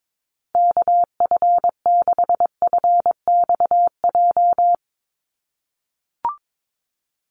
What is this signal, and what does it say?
Morse code, heard as a keyed tone, about 22 wpm, 700 Hz. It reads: KF6FXJ
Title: USA Call Signs 22wpm